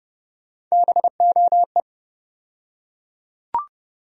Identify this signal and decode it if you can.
Morse code, heard as a keyed tone, about 30 wpm, 700 Hz. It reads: BOE
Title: USA Call Sign Postfixes - 3 Letters 30wpm